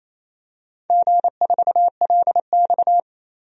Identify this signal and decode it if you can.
Morse code, heard as a keyed tone, about 28 wpm, 700 Hz. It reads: G4LX
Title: DXCC-100 Call Signs - 4 Characters - 28wpm